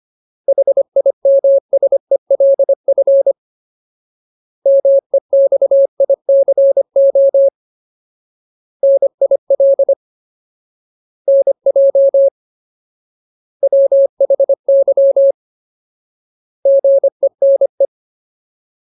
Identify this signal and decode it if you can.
Morse code, heard as a keyed tone, about 25 wpm, 550 Hz. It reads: HIMSELF MEXICO NIL NJ WHY GENE
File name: Ultimate Word List / 25WPM Ultimate